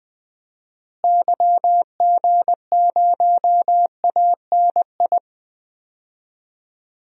Morse code, heard as a keyed tone, about 21 wpm, 700 Hz. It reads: YG0ANI
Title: DXCC-100 Call Signs - 6 Characters - Repeated 3x - 20wpm